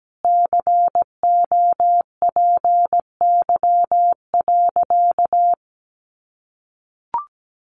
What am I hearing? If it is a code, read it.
Morse code, heard as a keyed tone, about 17 wpm, 700 Hz. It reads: COPY.
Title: Sentences from Top 1400 Words 17wpm